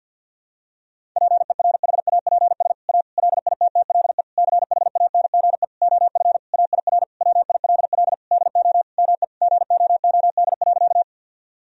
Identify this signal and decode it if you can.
Morse code, heard as a keyed tone, about 52 wpm, 700 Hz. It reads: PERHAPS A LITTLE CHANGE OF AIR WILL DO ME GOOD.
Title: Sentences from Top 1000 Words - Review Part 1 - 50wpm